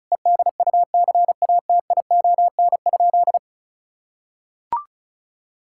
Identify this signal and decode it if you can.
Morse code, heard as a keyed tone, about 36 wpm, 700 Hz. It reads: EDUCATION?
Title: Sentences from Top 1000 Words - Review Part 6 - 35wpm